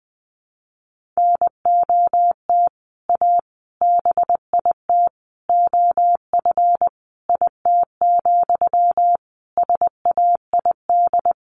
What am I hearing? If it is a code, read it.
Morse code, heard as a keyed tone, about 20 wpm, 700 Hz. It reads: NOT A BIT OF IT, SAID
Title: Sentences from Top 1400 Words 20wpm